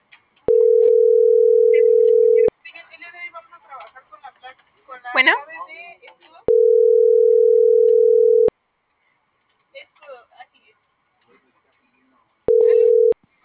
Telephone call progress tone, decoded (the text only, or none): ringback tone